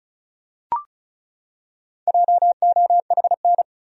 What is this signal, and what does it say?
Morse code, heard as a keyed tone, about 34 wpm, 700 Hz. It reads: JOHN
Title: USA Common Names - 35wpm